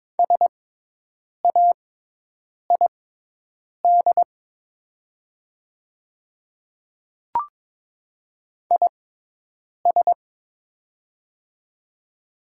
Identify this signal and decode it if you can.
Morse code, heard as a keyed tone, about 22 wpm, spaced as for 8 wpm, 700 Hz. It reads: SAID IS
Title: 6x Character Spacing - Top 200 Words - 22wpm